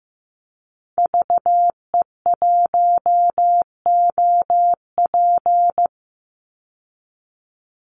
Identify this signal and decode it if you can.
Morse code, heard as a keyed tone, about 15 wpm, 700 Hz. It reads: VE1OP
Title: Call Signs Repeated 3x - 15wpm